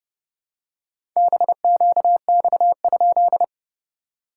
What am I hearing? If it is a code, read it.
Morse code, heard as a keyed tone, about 30 wpm, 690 Hz. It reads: BQX?
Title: SOTA QSO Practice 30wpm